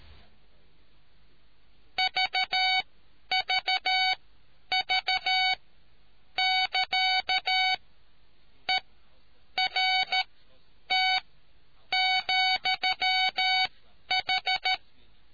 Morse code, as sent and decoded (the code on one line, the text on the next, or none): ...- ...- ...- -.-.- / . .-. - --..-- ....
VVV<KA> ERT,H